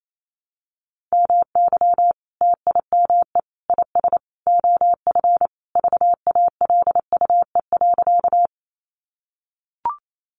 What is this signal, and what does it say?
Morse code, heard as a keyed tone, about 28 wpm, 700 Hz. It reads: MY TIME IS OF VALUE.
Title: Sentences from Top 600 Words 28wpm